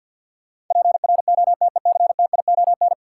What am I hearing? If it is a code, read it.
Morse code, heard as a keyed tone, about 53 wpm, 700 Hz. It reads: PROTECTION
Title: Sentences from Top 1600 Words - New Word Drill Alt - 50wpm